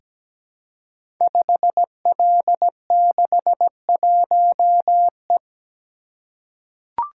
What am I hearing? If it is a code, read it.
Morse code, heard as a keyed tone, about 17 wpm, 700 Hz. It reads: 5L61E
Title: Code Groups - 5 Letter:Numbers - 17wpm